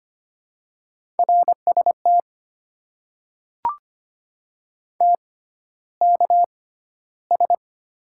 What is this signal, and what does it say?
Morse code, heard as a keyed tone, about 25 wpm, spaced as for 13 wpm, 700 Hz. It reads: RST TKS
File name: Top 90 QSO Elements - 6x Character Spacing - 25wpm